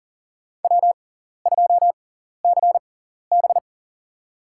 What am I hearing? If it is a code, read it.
Morse code, heard as a keyed tone, about 37 wpm, spaced as for 14 wpm, 700 Hz. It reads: W2CB
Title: USA Call Signs - 6x Character Spacing - 40wpm